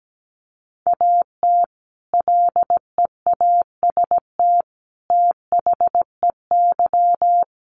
Morse code, heard as a keyed tone, about 17 wpm, 700 Hz. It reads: AT LEAST THEY
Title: Sentences from Top 1500 Words 17wpm